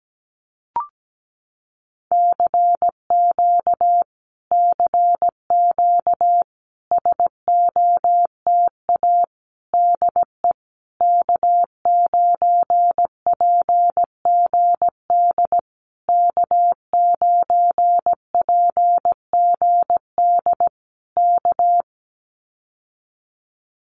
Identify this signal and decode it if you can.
Morse code, heard as a keyed tone, about 17 wpm, 700 Hz. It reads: CQ CQ SOTA DE K9PGD K9PGD K
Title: Common QSO Phrases Encore 17wpm